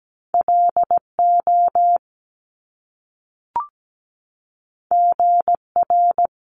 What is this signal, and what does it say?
Morse code, heard as a keyed tone, about 17 wpm, 700 Hz. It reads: LO GR
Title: Top Two Letter Combinations in Words 17wpm